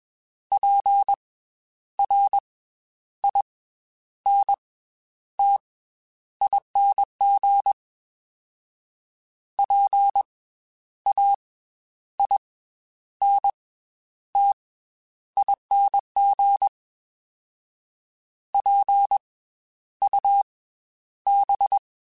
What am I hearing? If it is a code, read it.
Morse code, heard as a keyed tone, about 20 wpm, 800 Hz. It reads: P R I N T ING P A I N T ING P U B